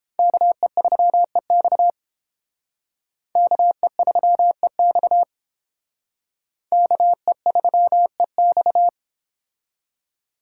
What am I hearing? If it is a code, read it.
Morse code, heard as a keyed tone, about 29 wpm, 700 Hz. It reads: KE3EX KE3EX KE3EX